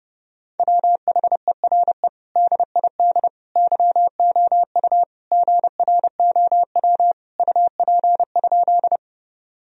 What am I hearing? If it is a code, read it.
Morse code, heard as a keyed tone, about 30 wpm, 700 Hz. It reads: WHERE DID YOU GROW UP?
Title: Sentences from Top 1000 Words - Review Part 1 - 30wpm